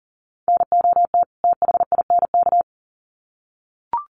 Morse code, heard as a keyed tone, about 40 wpm, 700 Hz. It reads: NOT THINK
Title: Sets of 2 Words 40wpm